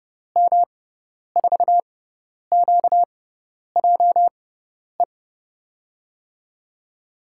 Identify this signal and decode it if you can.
Morse code, heard as a keyed tone, about 32 wpm, spaced as for 10 wpm, 700 Hz. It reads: M4QJE
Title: USA Call Signs - 6 Characters - 6x Character Spacing - 30wpm